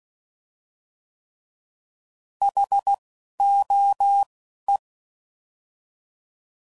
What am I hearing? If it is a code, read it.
Morse code, heard as a keyed tone, about 16 wpm, 800 Hz. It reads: H O E